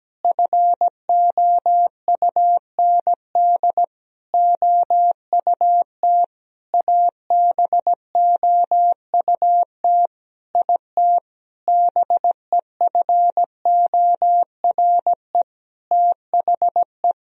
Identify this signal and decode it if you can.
Morse code, heard as a keyed tone, about 17 wpm, 700 Hz. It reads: FOUND OUT ABOUT IT BEFORE THE
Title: Sentences from Top 1000 Words - Review Part 3 - 17wpm